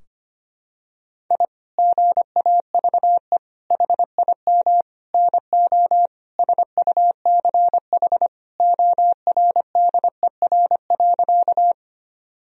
Morse code, heard as a keyed tone, about 25 wpm, 700 Hz. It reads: I GAVE HIM NO SUCH ORDER.
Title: Sentences from Top 1000 Words - Review Part 2 - 25wpm